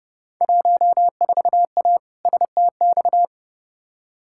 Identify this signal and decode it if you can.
Morse code, heard as a keyed tone, about 30 wpm, 700 Hz. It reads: 14A STX
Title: Field Day Exchanges Encore - 30wpm